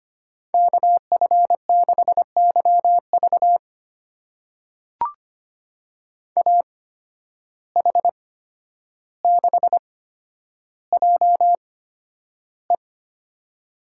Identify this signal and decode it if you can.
Morse code, heard as a keyed tone, about 25 wpm, spaced as for 11 wpm, 700 Hz. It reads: KF6YV AH6JE
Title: USA Call Signs - 5 Characters - 8x Character Spacing - 25wpm